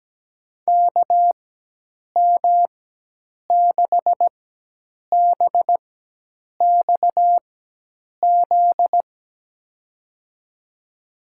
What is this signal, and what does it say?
Morse code, heard as a keyed tone, about 17 wpm, spaced as for 8 wpm, 700 Hz. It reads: KM6BXZ
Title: USA Call Signs - 6 Characters - 4x Character Spacing - 17wpm